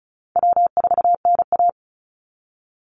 Morse code, heard as a keyed tone, about 34 wpm, 700 Hz. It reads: W4NA
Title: Call Signs Repeated 2x - 35wpm